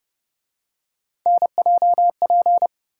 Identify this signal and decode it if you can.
Morse code, heard as a keyed tone, about 30 wpm, 700 Hz. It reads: NJP